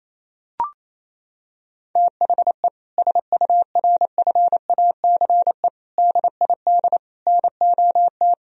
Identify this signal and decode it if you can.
Morse code, heard as a keyed tone, about 28 wpm, 700 Hz. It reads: THE SURFACE DID NOT
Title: Sentences from Top 1000 Words - Review Part 3 - 28wpm